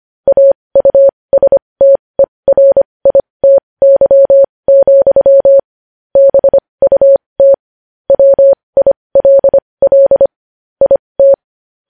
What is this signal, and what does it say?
Morse code, heard as a keyed tone, about 25 wpm, spaced as for 20 wpm, 550 Hz. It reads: AUSTERITY, BUT WILL IT
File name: podcast / News headlines-1x-25x20